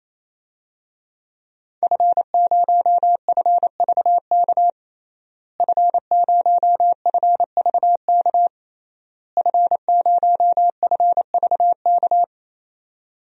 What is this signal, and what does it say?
Morse code, heard as a keyed tone, about 27 wpm, 700 Hz. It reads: F0FVK F0FVK F0FVK
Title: DXCC-100 Call Signs - 5 Characters - Repeated 3x - 28wpm